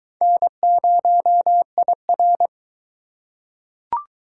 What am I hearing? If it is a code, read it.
Morse code, heard as a keyed tone, about 19 wpm, 700 Hz. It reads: N0IR